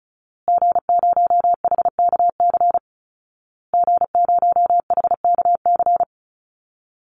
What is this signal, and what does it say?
Morse code, heard as a keyed tone, about 34 wpm, 700 Hz. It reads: G0HKC G0HKC
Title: Call Signs Repeated 2x - 35wpm